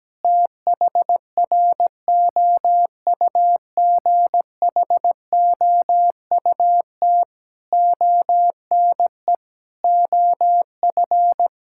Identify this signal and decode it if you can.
Morse code, heard as a keyed tone, about 17 wpm, 700 Hz. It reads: THROUGHOUT ONE OF
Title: Sentences from Top 1000 Words - Review Part 5 - 17wpm